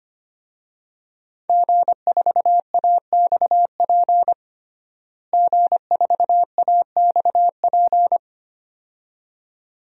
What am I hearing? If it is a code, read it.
Morse code, heard as a keyed tone, about 25 wpm, 700 Hz. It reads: G4AXP G4AXP
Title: DXCC-100 Call Signs - 5 Characters - Repeated 2x - 25wpm